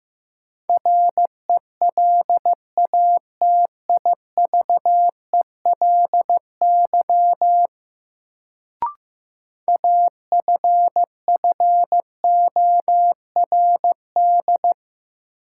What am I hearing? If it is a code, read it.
Morse code, heard as a keyed tone, about 15 wpm, 700 Hz. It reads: RELATIVELY AFFORD
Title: Sentences from Top 1100 Words - New Word Drill - 15wpm